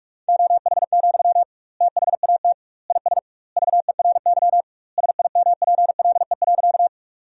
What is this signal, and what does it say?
Morse code, heard as a keyed tone, about 43 wpm, 700 Hz. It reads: OH, THAT IS VERY SIMPLE.
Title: 2x Word-Spacing - Sentences from Top 500 Words - 45wpm